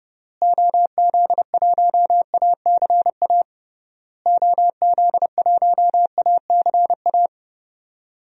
Morse code, heard as a keyed tone, about 30 wpm, 700 Hz. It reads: OZ1ACA OZ1ACA
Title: DXCC-100 Call Signs - 6 Characters - Repeated 3x - 30wpm